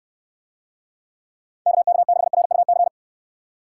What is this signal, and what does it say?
Morse code, heard as a keyed tone, about 69 wpm, 700 Hz. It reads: KK6RUC